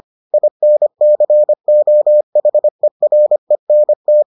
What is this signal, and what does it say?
Morse code, heard as a keyed tone, about 25 wpm, 600 Hz. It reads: INCOHERENT